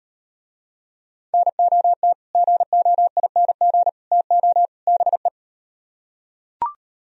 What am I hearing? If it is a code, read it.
Morse code, heard as a keyed tone, about 38 wpm, 700 Hz. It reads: NOT GOING TO BE